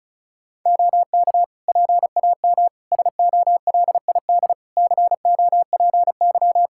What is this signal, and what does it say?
Morse code, heard as a keyed tone, about 34 wpm, 700 Hz. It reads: OK PAM SOLID COPY
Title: K1USN SST QSO Practice 35wpm